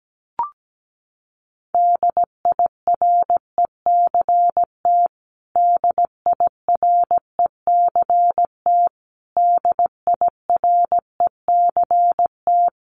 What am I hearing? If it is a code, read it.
Morse code, heard as a keyed tone, about 17 wpm, 700 Hz. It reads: DIRECT DIRECT DIRECT
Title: Top 500 Words Repeated 17wpm